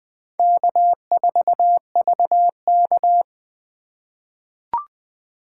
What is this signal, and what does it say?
Morse code, heard as a keyed tone, about 20 wpm, 700 Hz. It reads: K4VK
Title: USA Call Signs - 4 Characters - 2x Character Spacing - 20wpm